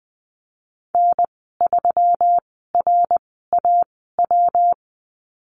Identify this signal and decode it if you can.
Morse code, heard as a keyed tone, about 20 wpm, spaced as for 15 wpm, 700 Hz. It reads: N3RAW
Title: 193 Lesson 31g - Letter v - Call Signs 20wpm